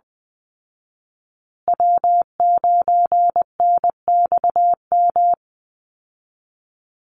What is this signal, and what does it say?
Morse code, heard as a keyed tone, about 20 wpm, 700 Hz. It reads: W9NXM
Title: Call Signs Repeated 3x - 20wpm